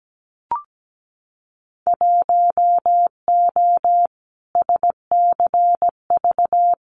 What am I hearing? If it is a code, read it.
Morse code, heard as a keyed tone, about 17 wpm, 700 Hz. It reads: 1O SCV
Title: Winter Field Day Exchanges 17wpm